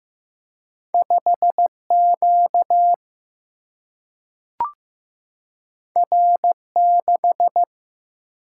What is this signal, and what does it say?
Morse code, heard as a keyed tone, about 15 wpm, 700 Hz. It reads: 5 Q R 6
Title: Single-Letter Single-Number 15wpm